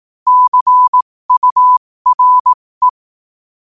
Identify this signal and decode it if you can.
Morse code, heard as a keyed tone, about 18 wpm, spaced as for 15 wpm, 1000 Hz. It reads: CURE
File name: qotdfiles / 15wpm